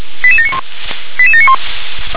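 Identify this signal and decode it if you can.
Five-tone selective call (ZVEI), heard as quick runs of five tones, tones of about 80 ms. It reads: 88791 88791
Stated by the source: ZVEI five-tone selective call received with an SDR.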